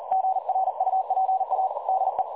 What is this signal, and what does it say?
Morse code, heard as a keyed tone, about 41 wpm, 760 Hz. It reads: GRUPA DE
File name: dlazaawansowanych / nagraniazpasma / websdr recording 3565.0kHz